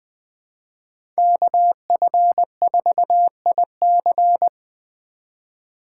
Morse code, heard as a keyed tone, about 20 wpm, 700 Hz. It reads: KF4IC
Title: USA Call Signs - 5 Characters - 3x Character Spacing - 20wpm